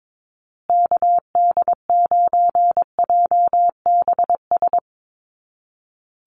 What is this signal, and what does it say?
Morse code, heard as a keyed tone, about 22 wpm, 700 Hz. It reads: KD9JBS
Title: USA Call Signs Repeated 3x 22wpm